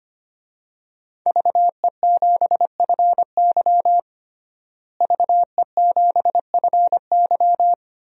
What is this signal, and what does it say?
Morse code, heard as a keyed tone, about 25 wpm, 700 Hz. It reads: VE7FY VE7FY
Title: DXCC-100 Call Signs - 5 Characters - Repeated 3x - 25wpm